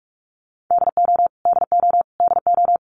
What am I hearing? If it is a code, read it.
Morse code, heard as a keyed tone, about 44 wpm, 700 Hz. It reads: DO DO DO
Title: Top 200 Words Repeated 45wpm